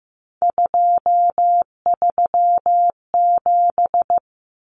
Morse code, heard as a keyed tone, about 15 wpm, 700 Hz. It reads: 237
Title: Numbers - 3-Digit 15wpm